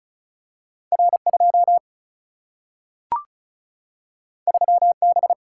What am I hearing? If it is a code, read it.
Morse code, heard as a keyed tone, about 36 wpm, 700 Hz. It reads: R2 3B
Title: Single-Letter Single-Number 35wpm